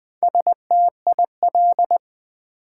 Morse code, heard as a keyed tone, about 20 wpm, 700 Hz. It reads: STIL
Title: Build A Word - 1 Letter At A Time - 3 to 5-Letter Words - 20wpm